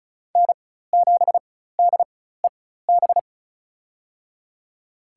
Morse code, heard as a keyed tone, about 36 wpm, spaced as for 16 wpm, 700 Hz. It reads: N7DEB
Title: USA Call Signs - 5 Characters - 4x Character Spacing - 35wpm